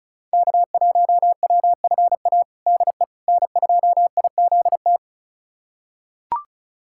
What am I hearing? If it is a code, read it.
Morse code, heard as a keyed tone, about 34 wpm, 700 Hz. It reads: K1WFA DE N2IZT
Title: USA Callsign DE Callsign - 5 Characters - 35wpm